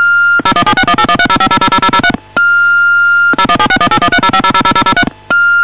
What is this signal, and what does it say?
DTMF, heard as telephone touch tones, tones of about 70 ms. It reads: 0176181305000003 0176181305000003